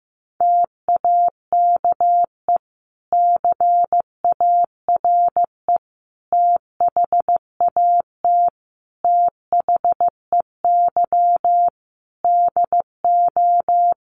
Morse code, heard as a keyed tone, about 15 wpm, 700 Hz. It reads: TAKE CARE THAT THEY DO
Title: Sentences from Top 1500 Words 15wpm